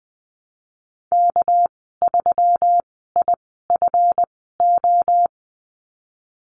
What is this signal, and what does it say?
Morse code, heard as a keyed tone, about 20 wpm, spaced as for 15 wpm, 700 Hz. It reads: K3IFO
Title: 207 Lesson 33g - Letter j - Call Signs 20wpm